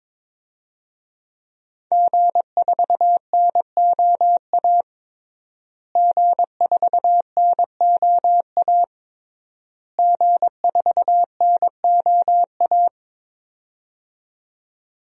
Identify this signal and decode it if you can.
Morse code, heard as a keyed tone, about 22 wpm, 700 Hz. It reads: G4NOA G4NOA G4NOA
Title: DXCC-100 Call Signs - 5 Characters - Repeated 3x - 22wpm